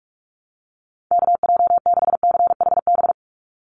Morse code, heard as a keyed tone, about 44 wpm, 700 Hz. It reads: KJ6CHB
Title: USA Call Signs 45wpm